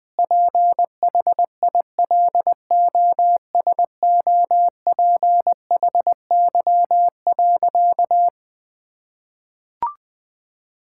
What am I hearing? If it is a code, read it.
Morse code, heard as a keyed tone, about 20 wpm, 700 Hz. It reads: PHILOSOPHY.
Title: Sentences from Top 1000 Words - Review Part 6 - 20wpm